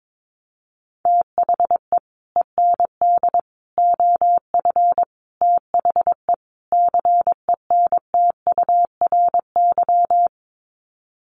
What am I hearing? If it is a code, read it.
Morse code, heard as a keyed tone, about 22 wpm, 700 Hz. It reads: THE END OF THE CENTURY
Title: Sets of 5 Words Encore2 22wpm